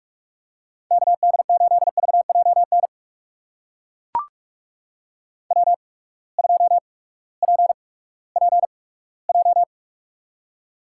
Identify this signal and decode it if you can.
Morse code, heard as a keyed tone, about 42 wpm, spaced as for 20 wpm, 700 Hz. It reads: KD8VJN W2PPJ
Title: USA Call Signs - 8x Character Spacing - 45wpm